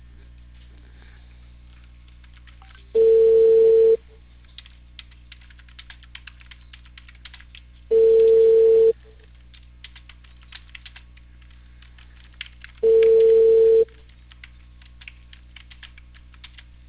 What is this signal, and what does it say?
Ringback tone (China, 450 Hz).